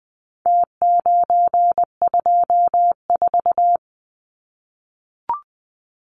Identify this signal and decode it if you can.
Morse code, heard as a keyed tone, about 20 wpm, 700 Hz. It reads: T924
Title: Numbers - Contest Style - 4-Digit 20wpm